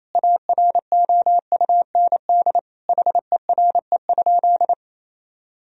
Morse code, heard as a keyed tone, about 28 wpm, 700 Hz. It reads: AROUND HERE?
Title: Sentences from Top 1000 Words - Review Part 3 - 28wpm